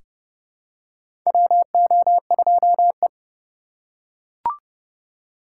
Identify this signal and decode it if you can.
Morse code, heard as a keyed tone, about 30 wpm, 700 Hz. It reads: WO2E
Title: K1USN SST QSO Practice - 30wpm